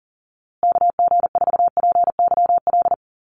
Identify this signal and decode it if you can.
Morse code, heard as a keyed tone, about 40 wpm, 700 Hz. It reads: KG4PYL
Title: USA Call Signs Repeated 2x 40wpm